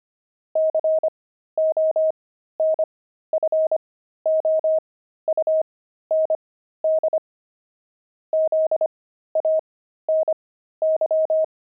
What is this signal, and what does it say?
Morse code, heard as a keyed tone, about 25 wpm, spaced as for 13 wpm, 630 Hz. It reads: CONFOUND ZANY
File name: pangram-fs-13-cs-25-zany